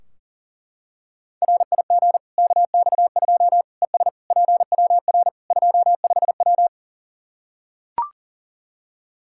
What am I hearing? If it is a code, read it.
Morse code, heard as a keyed tone, about 40 wpm, 700 Hz. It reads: RIG KX2 ES PWR 25W
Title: Practice QSO - Exchange 2 - Info - 40wpm